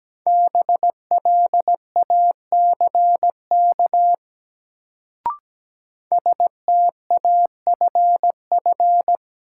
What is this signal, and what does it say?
Morse code, heard as a keyed tone, about 17 wpm, 700 Hz. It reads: BLACK STAFF
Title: Sets of 5 Words Encore - New Word Drill - 17wpm